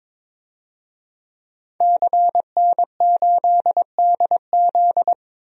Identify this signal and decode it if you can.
Morse code, heard as a keyed tone, about 22 wpm, 700 Hz. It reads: CN8DZ
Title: DXCC-100 Call Signs - 5 Characters - 22wpm